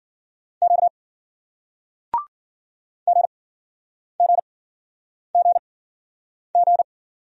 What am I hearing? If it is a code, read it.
Morse code, heard as a keyed tone, about 50 wpm, spaced as for 9 wpm, 700 Hz. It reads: Q GGGG